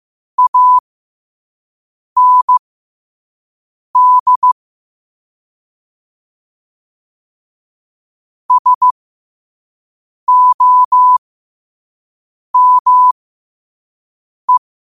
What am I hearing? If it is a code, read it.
Morse code, heard as a keyed tone, about 15 wpm, spaced as for 5 wpm, 1000 Hz. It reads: AND SOME
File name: qotdfiles / 5wpm